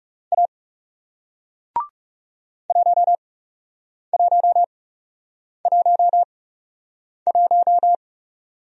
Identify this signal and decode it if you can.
Morse code, heard as a keyed tone, about 42 wpm, spaced as for 8 wpm, 700 Hz. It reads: A 1111